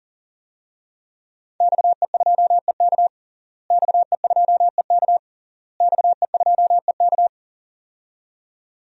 Morse code, heard as a keyed tone, about 40 wpm, 700 Hz. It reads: XE2EK XE2EK XE2EK
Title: DXCC-100 Call Signs - 5 Characters - Repeated 3x - 40wpm